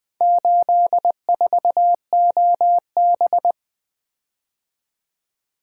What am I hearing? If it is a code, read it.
Morse code, heard as a keyed tone, about 24 wpm, 700 Hz. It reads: 84OB
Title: Speed-Racing - Four Letter-Number Combinations - 20wpm